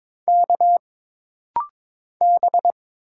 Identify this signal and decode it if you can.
Morse code, heard as a keyed tone, about 22 wpm, 700 Hz. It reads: K B
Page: This rapid-fire timing is geared to push your ICR (Instant Character Recognition) skills.